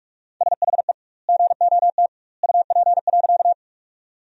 Morse code, heard as a keyed tone, about 46 wpm, 700 Hz. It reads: SHE GOT UP.
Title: 2x Word-Spacing - Sentences from Top 300 Words - 45wpm